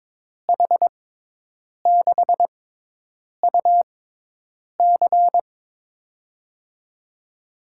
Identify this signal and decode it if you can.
Morse code, heard as a keyed tone, about 21 wpm, spaced as for 8 wpm, 700 Hz. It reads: H6UC
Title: USA Call Signs - 6x Character Spacing - 22wpm